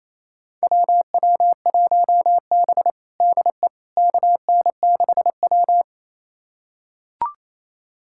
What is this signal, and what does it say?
Morse code, heard as a keyed tone, about 28 wpm, 700 Hz. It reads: WW1B DE KN6W
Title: USA Callsign DE Callsign - 4 Characters - 28wpm